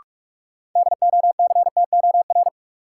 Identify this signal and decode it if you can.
Morse code, heard as a keyed tone, about 42 wpm, 700 Hz. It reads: DOCTOR
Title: Sentences from Top 1500 Words - New Word Drill Alt - 45wpm